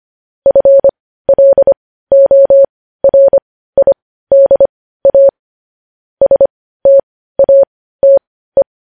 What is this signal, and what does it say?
Morse code, heard as a keyed tone, about 25 wpm, spaced as for 15 wpm, 550 Hz. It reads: FLORIDA STATE
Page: News Headlines in Morse Code at 15 WPM